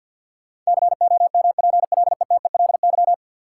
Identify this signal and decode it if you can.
Morse code, heard as a keyed tone, about 50 wpm, 700 Hz. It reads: COMPLETELY
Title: Sentences from Top 1100 Words - New Word Drill - 50wpm